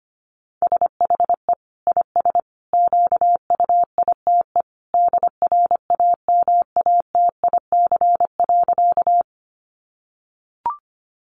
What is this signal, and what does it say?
Morse code, heard as a keyed tone, about 25 wpm, 700 Hz. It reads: SHE IS QUITE DRAMATIC.
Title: Sentences from Top 1300 Words 25wpm